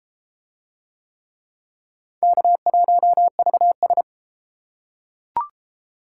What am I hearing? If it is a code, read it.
Morse code, heard as a keyed tone, about 33 wpm, 700 Hz. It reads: K1VS